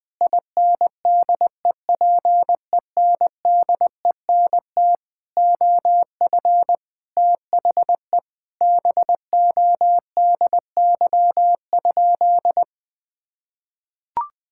Morse code, heard as a keyed tone, about 20 wpm, 700 Hz. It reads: INDEPENDENT OF THE BODY?
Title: Sentences from Top 1000 Words - Review Part 6 - 20wpm